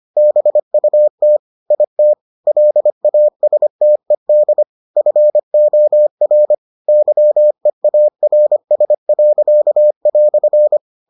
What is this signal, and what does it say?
Morse code, heard as a keyed tone, about 25 wpm, 600 Hz. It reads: BUT IT LASTED FOR YEARS."